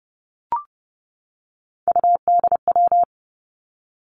Morse code, heard as a keyed tone, about 31 wpm, 700 Hz. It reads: UDW
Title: Three Challenging Letter-Number Combinations 30wpm